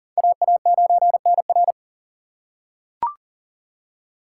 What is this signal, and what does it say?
Morse code, heard as a keyed tone, about 40 wpm, 700 Hz. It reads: AA9NR
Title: USA Call Signs - 4x Character Spacing - 40wpm